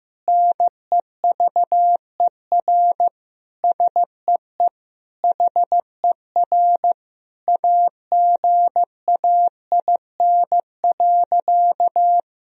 Morse code, heard as a keyed tone, about 15 wpm, 700 Hz. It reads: NEVER SEE HER AGAIN.
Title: Sentences from Top 1000 Words - Review Part 4 - 15wpm